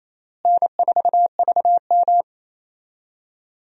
Morse code, heard as a keyed tone, about 28 wpm, 700 Hz. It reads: N4VM
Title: USA Call Signs - 4 Characters - Repeated 3x - 28wpm